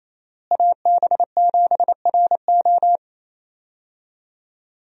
Morse code, heard as a keyed tone, about 28 wpm, 700 Hz. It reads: AB7RO
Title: USA Call Signs - 5 Characters - Repeated 3x - 28wpm